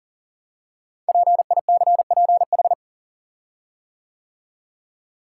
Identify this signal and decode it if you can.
Morse code, heard as a keyed tone, about 40 wpm, 700 Hz. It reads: PICPH
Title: Code Groups - 5 Letters - 40wpm